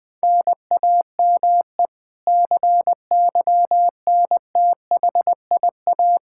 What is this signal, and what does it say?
Morse code, heard as a keyed tone, about 20 wpm, 700 Hz. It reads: NAME CYNTHIA
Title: K1USN SST QSO Practice 20wpm